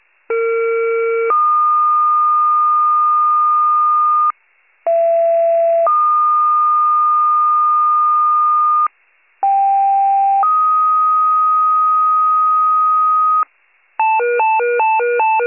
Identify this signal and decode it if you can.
Two-tone page, heard as a one-second tone, then a three-second tone, about 460 then 1220 Hz, about 670 then 1220 Hz, about 790 then 1250 Hz. It is heard from 0.3 to 13.4 s.